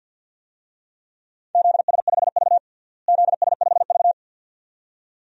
Morse code, heard as a keyed tone, about 52 wpm, 700 Hz. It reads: ZS5V ZS5V
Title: DXCC-100 Call Signs - 4 Characters - Repeated 2x - 50wpm